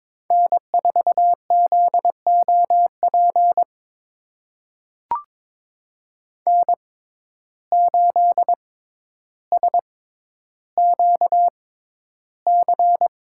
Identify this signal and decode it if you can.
Morse code, heard as a keyed tone, about 22 wpm, spaced as for 11 wpm, 700 Hz. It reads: N4ZOP N8SQC
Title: USA Call Signs - 5 Characters - 6x Character Spacing - 22wpm